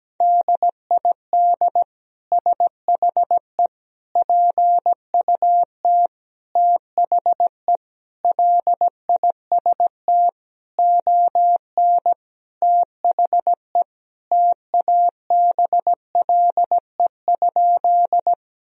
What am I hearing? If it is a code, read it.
Morse code, heard as a keyed tone, about 17 wpm, 700 Hz. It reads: DID SHE PUT THE LIST ON THE TABLE?
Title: Sentences from Top 1000 Words - Review Part 2 - 17wpm